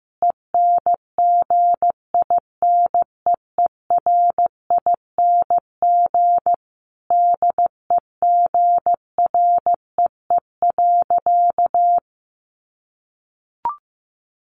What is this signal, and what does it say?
Morse code, heard as a keyed tone, about 15 wpm, 700 Hz. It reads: ENGINEERING DEGREE.
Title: Sentences from Top 1500 Words 15wpm